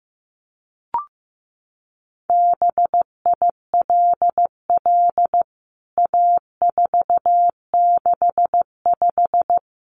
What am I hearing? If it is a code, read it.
Morse code, heard as a keyed tone, about 15 wpm, 700 Hz. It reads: BILL A465
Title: CWT Exchange 15wpm